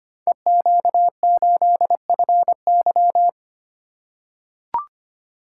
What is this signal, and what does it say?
Morse code, heard as a keyed tone, about 25 wpm, 700 Hz. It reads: EQ8FY
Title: Code Groups - 5 Letter:Numbers - 25wpm